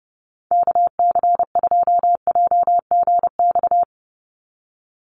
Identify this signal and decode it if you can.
Morse code, heard as a keyed tone, about 30 wpm, 700 Hz. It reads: KC2JGX
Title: USA Call Signs 30wpm